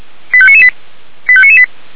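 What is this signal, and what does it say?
Five-tone selective call (ZVEI), heard as quick runs of five tones, tones of about 80 ms. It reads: 85998 85998
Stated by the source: ZVEI five-tone selective call received on an SDR.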